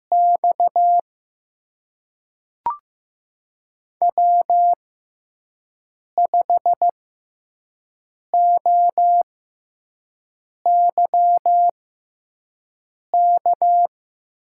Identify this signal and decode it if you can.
Morse code, heard as a keyed tone, about 15 wpm, spaced as for 5 wpm, 700 Hz. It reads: X W5OYK
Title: USA Call Signs - 6x Character Spacing - 15wpm